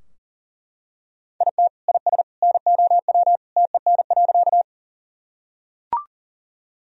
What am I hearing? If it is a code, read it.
Morse code, heard as a keyed tone, about 40 wpm, 700 Hz. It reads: IT IS NOW TEN.
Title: Sentences from Top 1000 Words - Review Part 2 - 40wpm